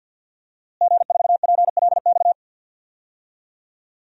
Morse code, heard as a keyed tone, about 49 wpm, 700 Hz. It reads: G4PLX
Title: DXCC-100 Call Signs - 5 Characters - Repeated 3x - 50wpm